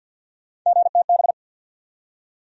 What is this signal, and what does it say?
Morse code, heard as a keyed tone, about 53 wpm, 700 Hz. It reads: GTB
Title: USA Call Sign Postfixes - 3 Letters 50wpm